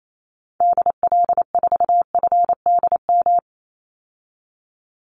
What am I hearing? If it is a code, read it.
Morse code, heard as a keyed tone, about 28 wpm, 700 Hz. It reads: DL4FDM
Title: Call Signs Repeated 2x - 28wpm